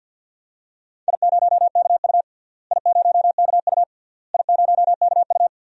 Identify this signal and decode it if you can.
Morse code, heard as a keyed tone, about 47 wpm, 700 Hz. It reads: I0KU I0KU I0KU
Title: DXCC-100 Call Signs - 4 Characters - Repeated 3x - 50wpm